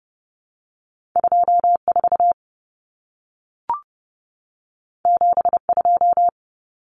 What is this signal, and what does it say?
Morse code, heard as a keyed tone, about 30 wpm, 700 Hz. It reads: 24 72
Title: Numbers - 2-Digit 30wpm